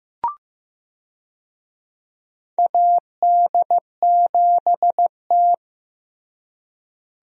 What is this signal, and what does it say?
Morse code, heard as a keyed tone, about 15 wpm, 700 Hz. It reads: AD7T
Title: USA Call Signs - 4 Characters - Repeated 2x - 15wpm